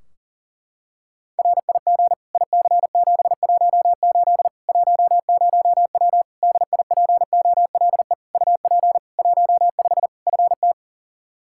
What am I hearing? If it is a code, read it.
Morse code, heard as a keyed tone, about 40 wpm, 700 Hz. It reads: RIG IC718 10W DIPOLE UP 15 FT
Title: Practice QSO - Exchange 2 - Info - 40wpm